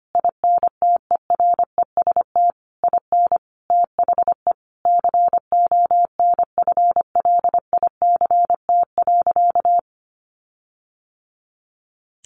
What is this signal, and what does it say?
Morse code, heard as a keyed tone, about 25 wpm, 700 Hz. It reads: INTEREST IN THE CONFLICT.
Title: Sentences from Top 1300 Words 25wpm